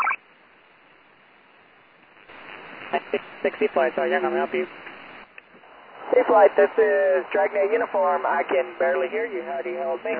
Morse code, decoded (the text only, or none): none